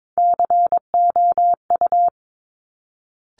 Morse code, heard as a keyed tone, about 22 wpm, 700 Hz. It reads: COU
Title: Top Three Letter Combinations in Words 22wpm